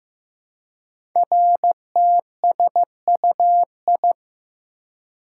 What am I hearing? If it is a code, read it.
Morse code, heard as a keyed tone, about 15 wpm, 700 Hz. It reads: RTSUI
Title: Code Groups - 5 Letters - 15wpm